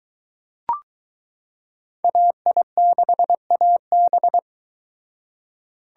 Morse code, heard as a keyed tone, about 22 wpm, 700 Hz. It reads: AI6AB